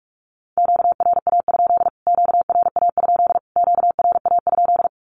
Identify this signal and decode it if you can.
Morse code, heard as a keyed tone, about 44 wpm, 700 Hz. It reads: QRA? QRA? QRA?
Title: Q-Codes Repeated 3x 45wpm